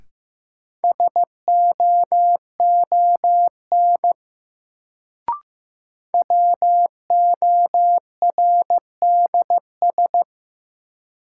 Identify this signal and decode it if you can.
Morse code, heard as a keyed tone, about 15 wpm, 700 Hz. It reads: SOON WORDS
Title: Sets of 3 Words - New Word Drill - 15wpm